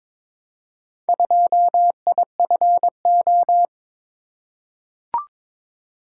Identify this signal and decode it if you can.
Morse code, heard as a keyed tone, about 22 wpm, 700 Hz. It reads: 2IFO
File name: practice / 22wpm / Four Letter-Number Combinations - 22wpm